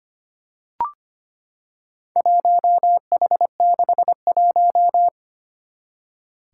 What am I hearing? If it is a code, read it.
Morse code, heard as a keyed tone, about 25 wpm, 700 Hz. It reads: 1H61
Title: Four Challenging Letter-Number Combinations 25wpm